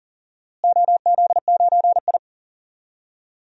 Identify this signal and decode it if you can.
Morse code, heard as a keyed tone, about 40 wpm, 700 Hz. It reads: OZ9I
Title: DXCC-100 Call Signs - 4 Characters - Repeated 3x - 40wpm